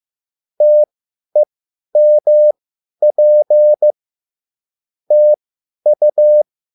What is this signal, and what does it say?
Morse code, heard as a keyed tone, about 15 wpm, spaced as for 10 wpm, 600 Hz. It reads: TEMP TU